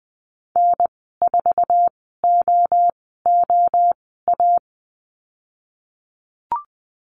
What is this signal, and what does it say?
Morse code, heard as a keyed tone, about 20 wpm, spaced as for 14 wpm, 700 Hz. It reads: N4OOA
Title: 055 Lesson 10g - Letter r - Call Signs 20wpm